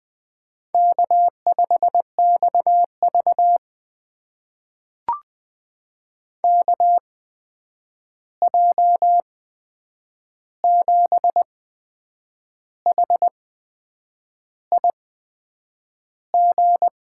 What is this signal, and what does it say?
Morse code, heard as a keyed tone, about 20 wpm, spaced as for 8 wpm, 700 Hz. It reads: K5XV KJ7HIG
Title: USA Call Signs - 8x Character Spacing - 20wpm